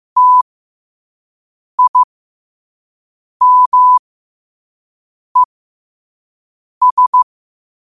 Morse code, heard as a keyed tone, about 15 wpm, spaced as for 5 wpm, 1000 Hz. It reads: TIMES
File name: 5wpm